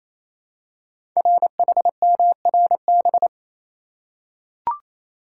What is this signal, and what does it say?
Morse code, heard as a keyed tone, about 27 wpm, 700 Hz. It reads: RHMRB
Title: Code Groups - 5 Letters - 28wpm